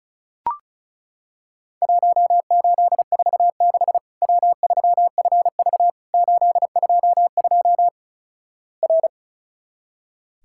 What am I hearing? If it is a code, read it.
Morse code, heard as a keyed tone, about 36 wpm, 700 Hz. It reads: 1846 W3FV 822 R
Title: Worked All Europe DX Contest - QTC Exchange - 35wpm